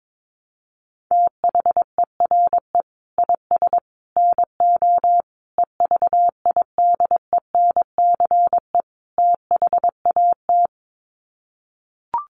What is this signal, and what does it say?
Morse code, heard as a keyed tone, about 22 wpm, 700 Hz. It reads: THERE IS NO EVIDENCE THAT
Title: Sets of 5 Words Encore 22wpm